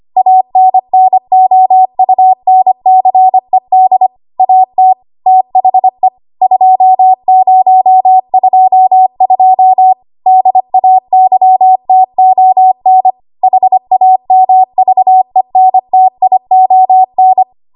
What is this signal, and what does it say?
Morse code, heard as a keyed tone, about 25 wpm, 750 Hz. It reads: ANNOUNCED AT THE 2022 DAYTON HAMVENTION
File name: Morse / 25WPM